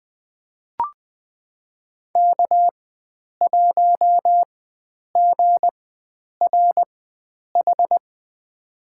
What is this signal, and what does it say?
Morse code, heard as a keyed tone, about 20 wpm, spaced as for 9 wpm, 700 Hz. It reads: K1GRH
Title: USA Call Signs - 5 Characters - 4x Character Spacing - 20wpm